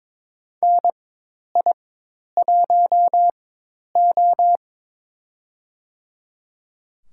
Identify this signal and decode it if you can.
Morse code, heard as a keyed tone, about 21 wpm, spaced as for 10 wpm, 700 Hz. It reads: NI1O
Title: USA Call Signs - 4 Characters - 4x Character Spacing - 22wpm